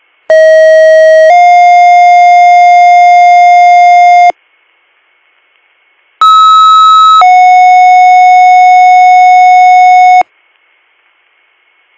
Two-tone page, heard as a one-second tone, then a three-second tone, about 640 then 710 Hz, about 1290 then 730 Hz. It is heard from 0.3 to 10.2 s.